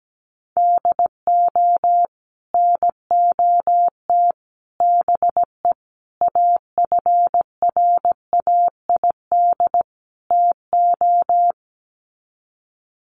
Morse code, heard as a keyed tone, about 17 wpm, 700 Hz. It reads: DO NOT BE AFRAID TO
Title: Sets of 5 Words Encore2 17wpm